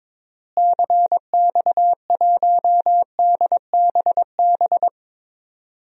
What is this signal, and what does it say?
Morse code, heard as a keyed tone, about 22 wpm, 700 Hz. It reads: CX1DBB
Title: DXCC-100 Call Signs - 6 Characters - Repeated 2x - 22wpm